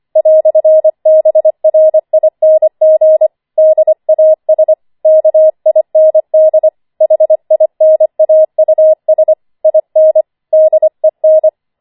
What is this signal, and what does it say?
Morse code, heard as a keyed tone, about 25 wpm, 630 Hz. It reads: "BRING DAS KIND HINAUS IN DEN
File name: 25wpm